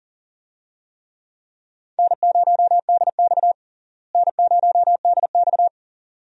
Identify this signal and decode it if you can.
Morse code, heard as a keyed tone, about 38 wpm, 700 Hz. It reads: N0DX N0DX
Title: USA Call Signs - 4 Characters - Repeated 2x - 40wpm